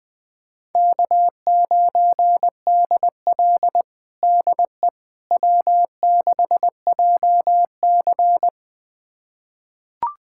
Morse code, heard as a keyed tone, about 20 wpm, 700 Hz. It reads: K9DL DE W6JC
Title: USA Callsign DE Callsign - 4 Characters - 20wpm